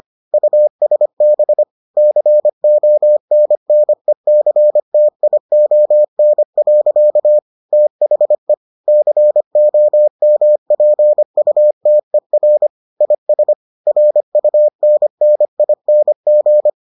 Morse code, heard as a keyed tone, about 25 wpm, 600 Hz. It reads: USB CONNECTION. THE COMPUTER IS RUNNING